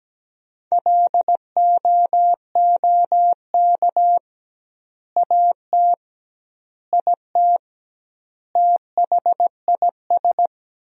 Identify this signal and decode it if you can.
Morse code, heard as a keyed tone, about 17 wpm, 700 Hz. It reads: LOOK AT IT THIS
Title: 2x Word-Spacing - Sentences from Top 100 Words - 17wpm